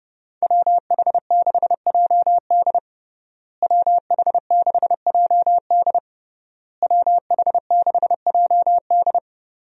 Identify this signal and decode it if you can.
Morse code, heard as a keyed tone, about 32 wpm, 700 Hz. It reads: WH6JD WH6JD WH6JD
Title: USA Call Signs - 5 Characters - Repeated 3x - 30wpm